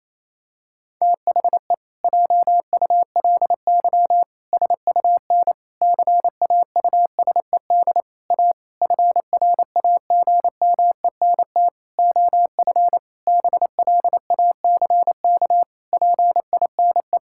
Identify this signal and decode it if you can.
Morse code, heard as a keyed tone, about 28 wpm, 700 Hz. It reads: THE JULY SUN CAUSED A FRAGMENT OF BLACK PINE
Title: Pangram Challenge v2 - 28wpm